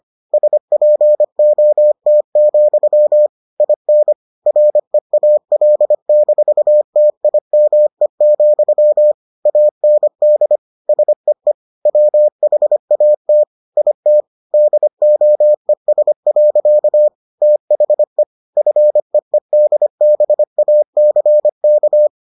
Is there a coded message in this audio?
Yes: Morse code from 0.3 to 22.1 s, heard as a keyed tone, about 25 wpm, 600 Hz. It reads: SPOT, IN REAL-TIME, AND SEE WHAT IT DOES. THE FEEDBACK